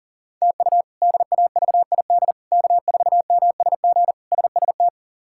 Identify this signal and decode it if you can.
Morse code, heard as a keyed tone, about 40 wpm, 690 Hz. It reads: TU DAVID K4MSG SST
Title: K1USN SST QSO Practice - 40wpm